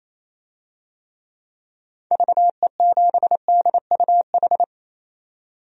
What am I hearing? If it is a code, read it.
Morse code, heard as a keyed tone, about 28 wpm, 700 Hz. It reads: VE7DUH
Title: DXCC-100 Call Signs - 6 Characters - 28wpm